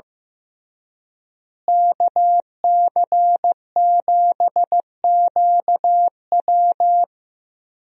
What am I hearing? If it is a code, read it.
Morse code, heard as a keyed tone, about 16 wpm, 700 Hz. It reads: KC7QW
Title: USA Call Signs - 5 Characters - Repeated 3x - 15wpm